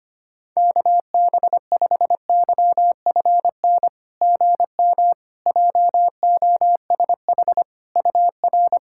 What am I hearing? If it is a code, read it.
Morse code, heard as a keyed tone, about 25 wpm, 690 Hz. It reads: KB5YFN GM JOSH UR
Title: SOTA QSO Practice 25wpm